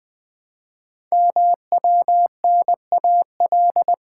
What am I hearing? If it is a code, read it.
Morse code, heard as a keyed tone, about 20 wpm, 700 Hz. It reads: MWNAL
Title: Code Groups - 5 Letters - 20wpm